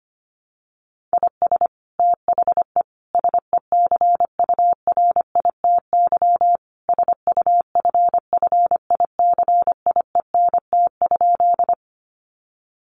Morse code, heard as a keyed tone, about 25 wpm, 700 Hz. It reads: IS THE SECURITY SUFFICIENT?
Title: Sentences from Top 1400 Words 25wpm